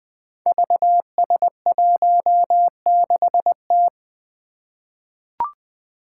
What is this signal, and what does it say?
Morse code, heard as a keyed tone, about 20 wpm, 700 Hz. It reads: VS16T
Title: Five Challenging Letter-Number Combinations 20wpm